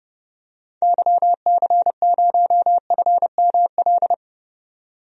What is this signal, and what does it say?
Morse code, heard as a keyed tone, about 32 wpm, 690 Hz. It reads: YC0FML
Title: DXCC-100 Call Signs - 6 Characters - 30wpm